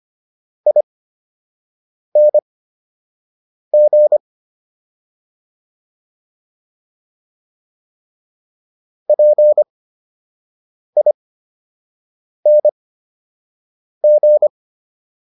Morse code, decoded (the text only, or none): ING PING